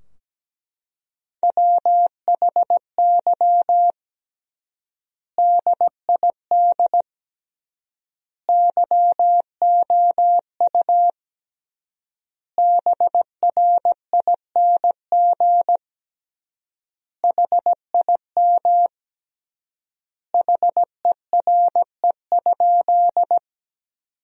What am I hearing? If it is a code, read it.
Morse code, heard as a keyed tone, about 17 wpm, 700 Hz. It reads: WHY DID YOU BRING HIM HERE?
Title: 3x Word-Spacing - Sentences from Top 500 Words - 17wpm